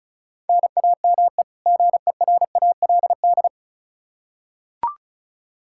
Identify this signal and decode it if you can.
Morse code, heard as a keyed tone, about 36 wpm, 700 Hz. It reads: NAME GERALD
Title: Practice QSO - Exchange 1 - Beginning - 35wpm